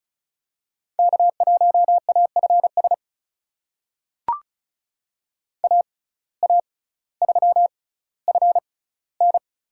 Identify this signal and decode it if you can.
Morse code, heard as a keyed tone, about 36 wpm, spaced as for 18 wpm, 700 Hz. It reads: K1AFS AA3FN
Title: USA Call Signs - 5 Characters - 6x Character Spacing - 35wpm